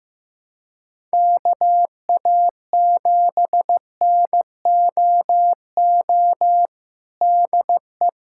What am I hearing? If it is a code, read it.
Morse code, heard as a keyed tone, about 15 wpm, 700 Hz. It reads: KA7NOO DE
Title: USA Callsign DE Callsign 15wpm